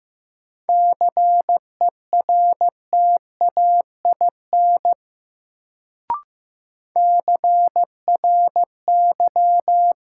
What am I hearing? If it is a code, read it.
Morse code, heard as a keyed tone, about 15 wpm, 700 Hz. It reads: CERTAIN CRY
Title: Sentences from Top 500 Words - New Word Drill - 15wpm